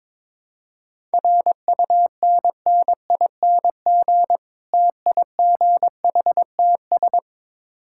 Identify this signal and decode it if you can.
Morse code, heard as a keyed tone, about 21 wpm, 700 Hz. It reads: RUNNING TIGHTS
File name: Difficult Words with Hints - 22wpm